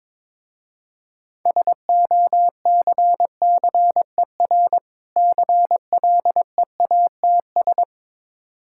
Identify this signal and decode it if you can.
Morse code, heard as a keyed tone, about 22 wpm, 700 Hz. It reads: SOCCER CLEATS
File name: Difficult Words with Hints - 22wpm